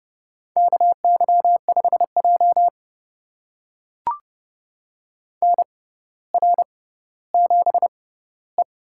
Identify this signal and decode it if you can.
Morse code, heard as a keyed tone, about 32 wpm, spaced as for 16 wpm, 700 Hz. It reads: KY5J NR7E
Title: USA Call Signs - 4 Characters - 6x Character Spacing - 30wpm